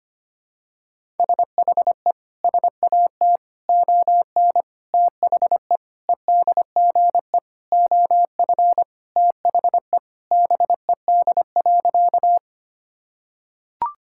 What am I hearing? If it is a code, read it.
Morse code, heard as a keyed tone, about 25 wpm, 700 Hz. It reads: SHE SAT ON THE EDGE OF THE BED.
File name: practice / 25wpm / Sentences with 4-Word Phrases - 25wpm